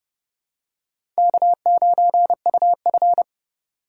Morse code, heard as a keyed tone, about 30 wpm, 700 Hz. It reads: K9UF
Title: USA Call Signs - 4 Characters - 3x Character Spacing - 30wpm